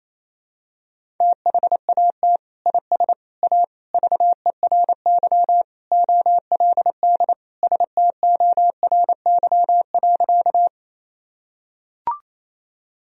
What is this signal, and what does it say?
Morse code, heard as a keyed tone, about 28 wpm, 700 Hz. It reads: THAT IS A VERY OLD STORY.
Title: Sentences from Top 1000 Words - Review Part 1 - 28wpm